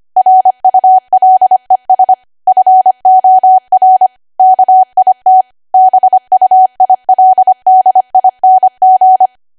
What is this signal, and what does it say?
Morse code, heard as a keyed tone, about 25 wpm, 750 Hz. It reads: RULES FOR KIT BUILDING
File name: cwa / W1AW-Level3 / 25WPM